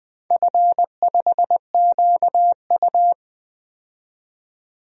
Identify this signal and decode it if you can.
Morse code, heard as a keyed tone, about 21 wpm, 700 Hz. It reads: F5QU
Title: DXCC-100 Call Signs - 4 Characters - 20wpm